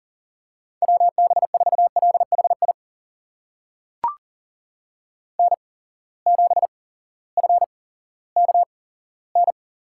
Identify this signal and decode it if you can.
Morse code, heard as a keyed tone, about 39 wpm, spaced as for 18 wpm, 700 Hz. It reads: WB4LHI N7FKN
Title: USA Call Signs - 8x Character Spacing - 40wpm